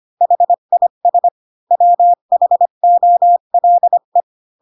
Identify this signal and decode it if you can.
Morse code, heard as a keyed tone, about 25 wpm, 700 Hz. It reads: HIS WHOLE